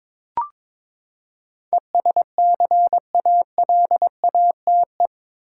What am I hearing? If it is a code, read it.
Morse code, heard as a keyed tone, about 22 wpm, 700 Hz. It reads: ESCALATE
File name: practice / 22wpm / Eight Letter Words - 22wpm